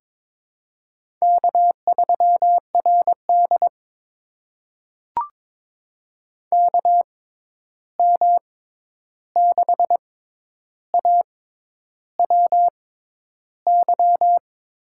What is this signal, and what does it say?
Morse code, heard as a keyed tone, about 22 wpm, spaced as for 10 wpm, 700 Hz. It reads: K3RD KM6AWY
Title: USA Call Signs - 6x Character Spacing - 22wpm